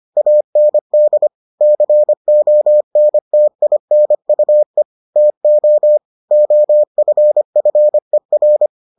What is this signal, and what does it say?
Morse code, heard as a keyed tone, about 25 wpm, 600 Hz. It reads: AND CONTINUE TO OFFER